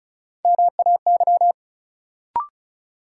Morse code, heard as a keyed tone, about 34 wpm, 700 Hz. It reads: MAY
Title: Sentences from Top 100 Words - New Word Drill - 35wpm